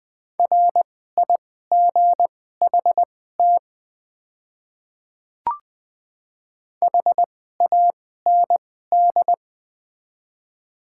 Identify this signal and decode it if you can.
Morse code, heard as a keyed tone, about 21 wpm, spaced as for 15 wpm, 700 Hz. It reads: RIGHT HAND
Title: 2x Character Spacing - Top 300 Words - 20wpm